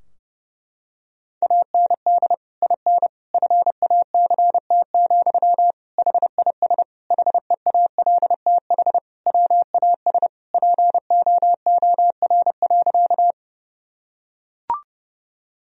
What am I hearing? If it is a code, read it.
Morse code, heard as a keyed tone, about 30 wpm, 700 Hz. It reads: AND IN FACT, HIS HEALTH WAS POOR.
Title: Sentences from Top 1000 Words - Review Part 3 - 30wpm